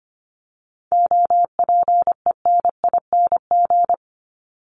Morse code, heard as a keyed tone, about 25 wpm, 700 Hz. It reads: OPENING
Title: ING Suffix - Top 100 Words with Base Words 25wpm